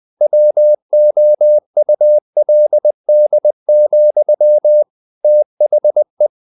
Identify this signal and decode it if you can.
Morse code, heard as a keyed tone, about 20 wpm, 600 Hz. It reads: WOULD, THE
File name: the old man and the sea / the old man and the sea part 1 20wpm